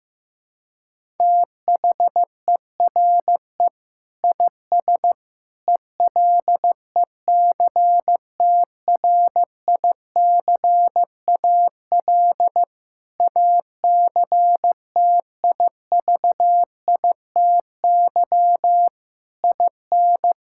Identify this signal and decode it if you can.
Morse code, heard as a keyed tone, about 14 wpm, 700 Hz. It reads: THERE IS ELECTRICAL ACTIVITY IN
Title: Sentences from Top 1000 Words - Review Part 6 - 15wpm